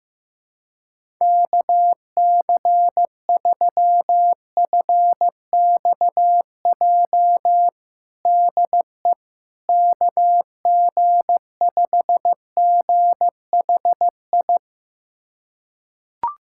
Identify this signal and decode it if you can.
Morse code, heard as a keyed tone, about 15 wpm, 700 Hz. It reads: KC3FXJ DE KG5GHI
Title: USA Callsign DE Callsign 15wpm